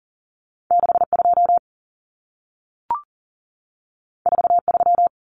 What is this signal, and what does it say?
Morse code, heard as a keyed tone, about 40 wpm, 700 Hz. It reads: 62 43
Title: Common Cut Numbers - 2-Digit 40wpm